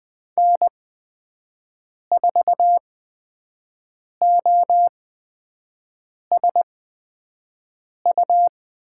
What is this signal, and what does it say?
Morse code, heard as a keyed tone, about 20 wpm, spaced as for 5 wpm, 700 Hz. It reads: N4OSU
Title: USA Call Signs - 5 Characters - 8x Character Spacing - 20wpm